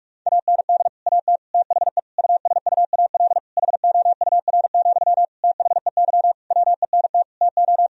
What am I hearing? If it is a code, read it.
Morse code, heard as a keyed tone, about 46 wpm, 700 Hz. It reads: AND AT THE USUAL HOUR, THEY WENT TO
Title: Sentences from Top 1000 Words - Review Part 2 - 45wpm